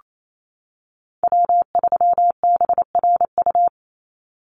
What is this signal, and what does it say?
Morse code, heard as a keyed tone, about 28 wpm, 700 Hz. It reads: W3BRU
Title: USA Call Signs 28wpm